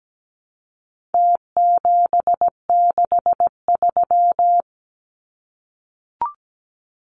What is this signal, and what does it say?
Morse code, heard as a keyed tone, about 17 wpm, 700 Hz. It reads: T763
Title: Numbers - Contest Style - 4-Digit 17wpm